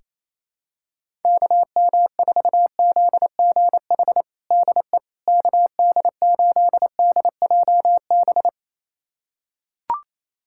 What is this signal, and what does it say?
Morse code, heard as a keyed tone, about 28 wpm, 700 Hz. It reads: KM4ZGH DE KD8DJB
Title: USA Callsign DE Callsign 28wpm